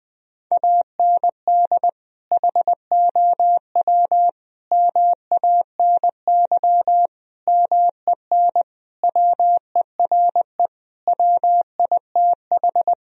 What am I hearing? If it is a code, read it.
Morse code, heard as a keyed tone, about 21 wpm, 700 Hz. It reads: AND HOW MANY MEN WERE WITH
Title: Sentences from Top 1000 Words - Review Part 1 - 20wpm